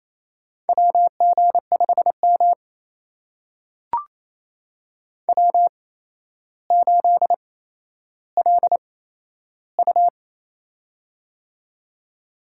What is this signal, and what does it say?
Morse code, heard as a keyed tone, about 28 wpm, spaced as for 12 wpm, 700 Hz. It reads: WG5M W8LU
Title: USA Call Signs - 4 Characters - 8x Character Spacing - 28wpm